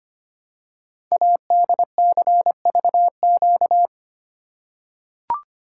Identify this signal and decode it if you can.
Morse code, heard as a keyed tone, about 25 wpm, 700 Hz. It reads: ADCVQ
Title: Code Groups - 5 Letters - 25wpm